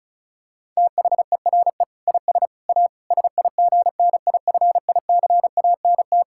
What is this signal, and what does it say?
Morse code, heard as a keyed tone, about 36 wpm, 690 Hz. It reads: THERE IS A SIGNIFICANT
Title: Sentences from Top 1000 Words - Review Part 6 - 35wpm